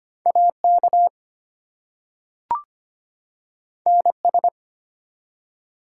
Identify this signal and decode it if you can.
Morse code, heard as a keyed tone, about 25 wpm, 700 Hz. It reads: AK NS
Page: morse-code-ninja-practice-25wpm